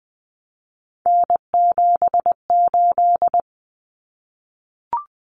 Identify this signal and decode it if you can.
Morse code, heard as a keyed tone, about 20 wpm, 700 Hz. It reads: N78
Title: Common Cut Numbers - 3-Digit 20wpm